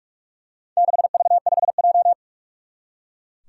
Morse code, heard as a keyed tone, about 41 wpm, 700 Hz. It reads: 6V5J
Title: Four Challenging Letter-Number Combinations 45wpm